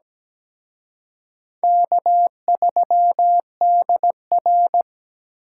Morse code, heard as a keyed tone, about 20 wpm, 700 Hz. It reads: K3DR
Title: Speed Racing - USA 1x2 Call Signs 17wpm